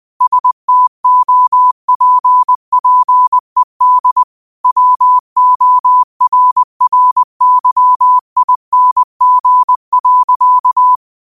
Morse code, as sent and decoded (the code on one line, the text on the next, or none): ... - --- .--. .--. -.. / .-- --- .-. .-. -.-- .. -. --. .-.-.-
STOPPED WORRYING.